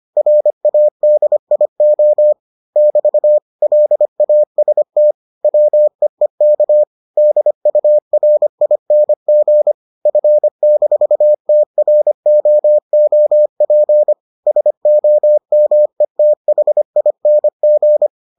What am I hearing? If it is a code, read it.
Morse code, heard as a keyed tone, about 25 wpm, 600 Hz. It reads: RADIO <BT> LAST WEEK DURING F-TROOP SOMETHING